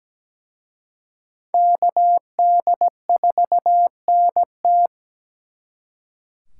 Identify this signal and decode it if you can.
Morse code, heard as a keyed tone, about 20 wpm, 700 Hz. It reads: KD4NT
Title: Speed Racing - USA 2x2 Call Signs 17wpm